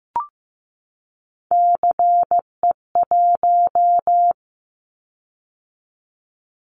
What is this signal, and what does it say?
Morse code, heard as a keyed tone, about 15 wpm, 700 Hz. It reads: CE1
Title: Callsign Prefixes for Top 100 DXCC - 15wpm